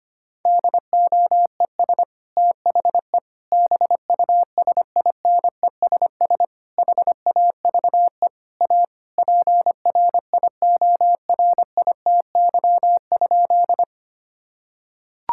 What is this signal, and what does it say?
Morse code, heard as a keyed tone, about 25 wpm, 700 Hz. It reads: DOES THE BUSINESS HAVE A PRIORITY?
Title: Sentences from Top 1600 Words 25wpm